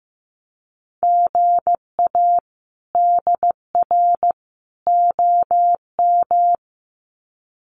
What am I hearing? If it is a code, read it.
Morse code, heard as a keyed tone, about 15 wpm, 700 Hz. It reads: GA DR OM
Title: Common QSO Phrases 15wpm